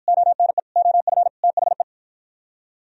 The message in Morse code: ONE OF THE

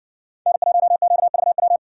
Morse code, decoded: N1QVW

